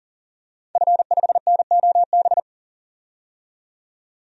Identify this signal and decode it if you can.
Morse code, heard as a keyed tone, about 40 wpm, 700 Hz. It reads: F5NOB
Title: DXCC-100 Call Signs - 5 Characters - 40wpm